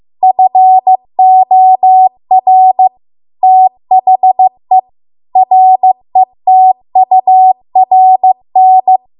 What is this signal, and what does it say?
Morse code, heard as a keyed tone, about 15 wpm, 750 Hz. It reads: FOR THE RETURN